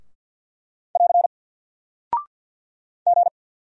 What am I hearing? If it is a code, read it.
Morse code, heard as a keyed tone, about 48 wpm, 700 Hz. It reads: <AR> G
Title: Single Character (Letter:Number:Punctuation:Prosign) Alt 50wpm